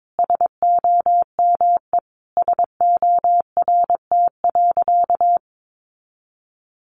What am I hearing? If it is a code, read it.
Morse code, heard as a keyed tone, about 22 wpm, 700 Hz. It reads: SOME SORT.
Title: Sentences from Top 1300 Words 22wpm